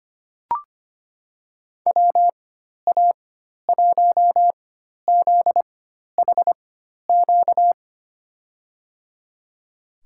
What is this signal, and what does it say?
Morse code, heard as a keyed tone, about 25 wpm, spaced as for 12 wpm, 700 Hz. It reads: WA1ZHQ
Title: USA Call Signs - 6 Characters - 4x Character Spacing - 25wpm